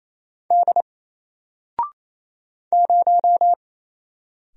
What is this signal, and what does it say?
Morse code, heard as a keyed tone, about 28 wpm, 700 Hz. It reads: D 0
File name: Single-Letter-Number-28wpm